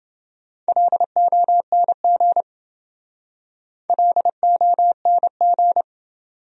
Morse code, heard as a keyed tone, about 27 wpm, 700 Hz. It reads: LONG LONG